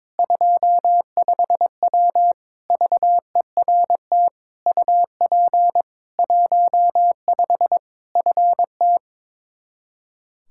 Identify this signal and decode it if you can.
Morse code, heard as a keyed tone, about 22 wpm, 700 Hz. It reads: 25W VERT UP 15 FT
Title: K1USN SST QSO Practice 22wpm